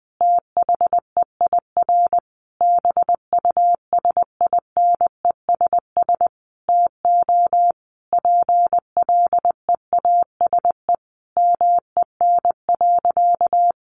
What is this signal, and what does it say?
Morse code, heard as a keyed tone, about 20 wpm, 700 Hz. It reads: THEIR BUSINESS TO PLEASE MEN.
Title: Sentences from Top 1500 Words 20wpm